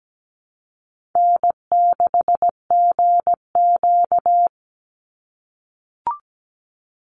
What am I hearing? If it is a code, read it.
Morse code, heard as a keyed tone, about 17 wpm, 700 Hz. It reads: N6GQ
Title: Easy Call Signs - 17wpm